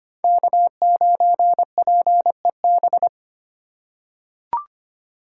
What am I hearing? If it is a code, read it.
Morse code, heard as a keyed tone, about 25 wpm, 700 Hz. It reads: K9PEB
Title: USA Call Signs - 5 Characters - 4x Character Spacing - 25wpm